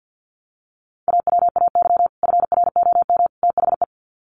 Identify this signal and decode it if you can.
Morse code, heard as a keyed tone, about 49 wpm, 700 Hz. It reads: AWAY FROM THE
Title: Sets of 3 Words 50wpm